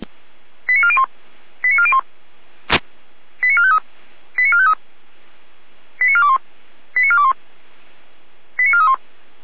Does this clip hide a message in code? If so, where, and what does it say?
Yes: five-tone selective call (ZVEI) from 0.7 to 8.9 s, heard as quick runs of five tones, tones of about 70 ms. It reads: 89401 89401 89453 89453 89421 89421 89431